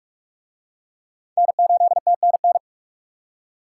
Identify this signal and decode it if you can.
Morse code, heard as a keyed tone, about 42 wpm, 700 Hz. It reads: N8TNN